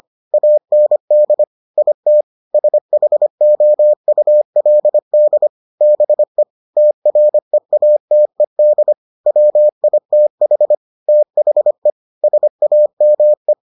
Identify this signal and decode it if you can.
Morse code, heard as a keyed tone, about 25 wpm, 600 Hz. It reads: AND IT SHOULD BE TREATED WITH THE SAME